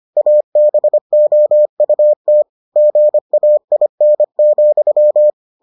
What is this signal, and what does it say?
Morse code, heard as a keyed tone, about 25 wpm, 600 Hz. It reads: ABOUT GAIN,